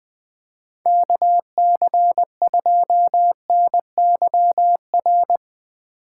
Morse code, heard as a keyed tone, about 20 wpm, 700 Hz. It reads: KC2NYR
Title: USA Call Signs - 8x Character Spacing - 20wpm